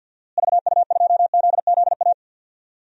Morse code, heard as a keyed tone, about 46 wpm, 700 Hz. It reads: VU2ZZA
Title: DXCC-100 Call Signs - 6 Characters - Repeated 3x - 50wpm